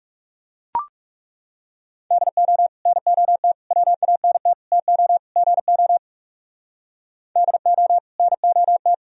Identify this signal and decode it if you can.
Morse code, heard as a keyed tone, about 37 wpm, 700 Hz. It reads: DO NOT WANT TO GO DO NOT
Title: Speed-Racing - Sets of 5 Words 30wpm